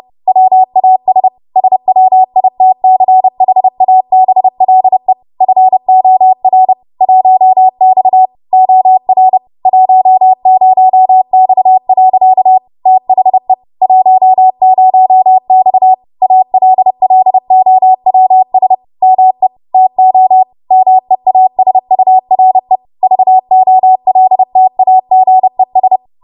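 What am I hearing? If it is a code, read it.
Morse code, heard as a keyed tone, about 30 wpm, 750 Hz. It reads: WAS SWITCHABLE FOR 1X OR 10X. THE 10X ALLOWS ME TO MEASURE VOLTAGES
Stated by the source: W1AW Web Code Practice Files, 30 WPM run from ARRL; text from April 2023 QST.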